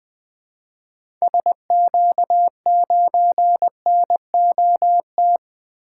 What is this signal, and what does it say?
Morse code, heard as a keyed tone, about 21 wpm, 700 Hz. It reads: SQ9NOT
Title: 2023 SOTA Activator Call Signs - 20wpm